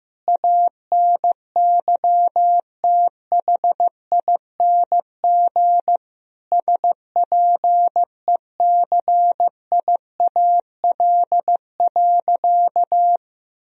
Morse code, heard as a keyed tone, about 15 wpm, 700 Hz. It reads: ANYTHING SPECIAL.
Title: Sentences from Top 1000 Words - Review Part 3 - 15wpm